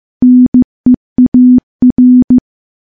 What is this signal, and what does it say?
Morse code, heard as a keyed tone, about 16 wpm, 250 Hz. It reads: NEAR
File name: morse-training-15WPM